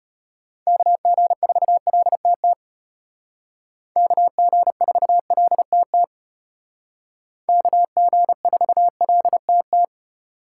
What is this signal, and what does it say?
Morse code, heard as a keyed tone, about 32 wpm, 700 Hz. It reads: KG4LTT KG4LTT KG4LTT